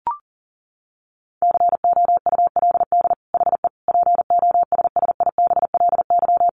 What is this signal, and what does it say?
Morse code, heard as a keyed tone, about 40 wpm, 700 Hz. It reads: COULD HE POSSIBLY
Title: Sentences from Top 1300 Words 40wpm